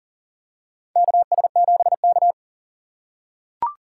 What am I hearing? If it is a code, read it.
Morse code, heard as a keyed tone, about 40 wpm, 700 Hz. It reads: KS7K
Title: USA Call Signs - 4x Character Spacing - 40wpm